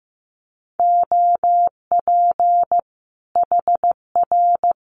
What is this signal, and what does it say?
Morse code, heard as a keyed tone, about 15 wpm, 700 Hz. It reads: OP HR
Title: Common QSO Phrases 15wpm